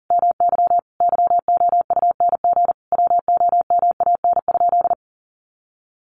Morse code, heard as a keyed tone, about 40 wpm, 700 Hz. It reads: MY YOUNG WOMAN?
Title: Sentences from Top 1100 Words 40wpm